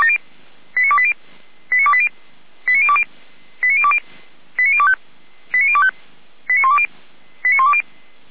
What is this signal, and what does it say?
Five-tone selective call (ZVEI), heard as quick runs of five tones, tones of about 70 ms. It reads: 89280 89280 89020 89020 89025 89025 89120 89120
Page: Server at audio.f85360.feuerwehr-orlamuende.org